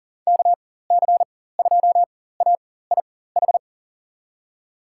Morse code, heard as a keyed tone, about 40 wpm, spaced as for 19 wpm, 700 Hz. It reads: KC2AIH
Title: USA Call Signs - 6 Characters - 4x Character Spacing - 40wpm